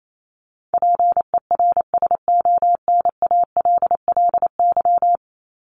Morse code, heard as a keyed tone, about 28 wpm, 700 Hz. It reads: PERSONALLY
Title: Top 1251-1500 Words 28wpm